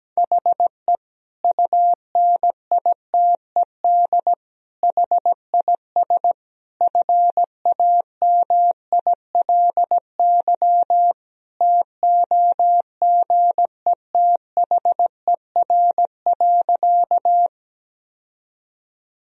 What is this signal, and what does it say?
Morse code, heard as a keyed tone, about 17 wpm, 700 Hz. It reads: HE UNITED HIS FAMILY TOGETHER.
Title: Sentences from Top 1000 Words - Review Part 4 - 17wpm